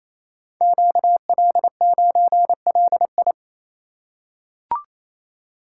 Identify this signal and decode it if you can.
Morse code, heard as a keyed tone, about 27 wpm, 690 Hz. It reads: QL9LI
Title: Code Groups - 5 Letter:Numbers - 28wpm